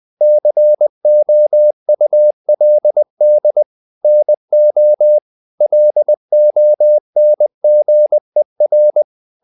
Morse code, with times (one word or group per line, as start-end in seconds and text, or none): 0.2-3.6 COULD
4.0-5.2 NO
5.6-9.0 LONGER